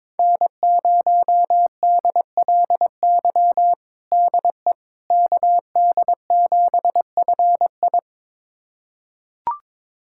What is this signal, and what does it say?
Morse code, heard as a keyed tone, about 22 wpm, 700 Hz. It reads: N0DLY DE KD7FI
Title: USA Callsign DE Callsign - 5 Characters - 22wpm